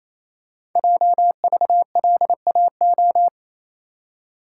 Morse code, heard as a keyed tone, about 27 wpm, 700 Hz. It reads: JVLAO
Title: Code Groups - 5 Letter:Numbers - 28wpm